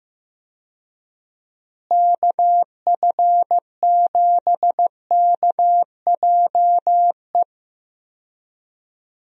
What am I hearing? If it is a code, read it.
Morse code, heard as a keyed tone, about 15 wpm, 700 Hz. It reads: KF7KJE
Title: USA Call Signs - 6 Characters - Repeated 2x - 15wpm